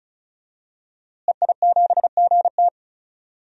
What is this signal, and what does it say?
Morse code, heard as a keyed tone, about 34 wpm, 700 Hz. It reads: EI7GT
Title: DXCC-100 Call Signs - 5 Characters - Repeated 3x - 35wpm